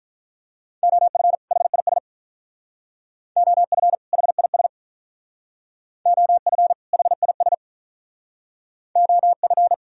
Morse code, heard as a keyed tone, about 46 wpm, 700 Hz. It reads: OF HIS OF HIS OF HIS OF